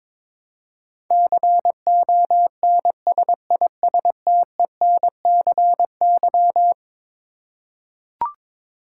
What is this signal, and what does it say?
Morse code, heard as a keyed tone, about 22 wpm, 700 Hz. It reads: CONSISTENCY
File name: practice / 22wpm / Eleven Letter Words - 22wpm